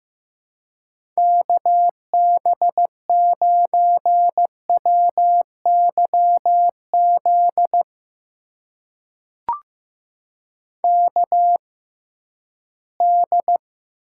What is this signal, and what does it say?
Morse code, heard as a keyed tone, about 15 wpm, spaced as for 11 wpm, 700 Hz. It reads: KB9WYZ KD
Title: USA Call Signs - 6 Characters - 6x Character Spacing - 15wpm